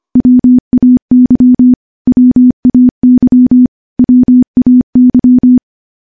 Morse code, heard as a keyed tone, about 25 wpm, 250 Hz. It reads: WAY WAY WAY
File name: morse / morse-training-25WPM